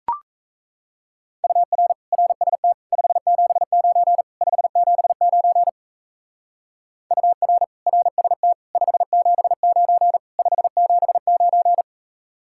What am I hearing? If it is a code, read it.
Morse code, heard as a keyed tone, about 34 wpm, 700 Hz. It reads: UR RST 579 579 UR RST 579 579
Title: Speed-Racing Common QSO Phrases - 28wpm